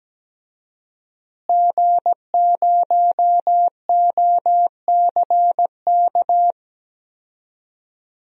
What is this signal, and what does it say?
Morse code, heard as a keyed tone, about 17 wpm, 700 Hz. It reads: G0OCK
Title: DXCC-100 Call Signs - 5 Characters - Repeated 2x - 17wpm